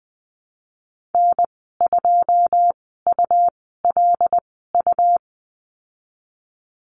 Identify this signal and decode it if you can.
Morse code, heard as a keyed tone, about 20 wpm, spaced as for 15 wpm, 700 Hz. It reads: N2ULU
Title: 161 Lesson 26g - Letter g - Call Signs 20wpm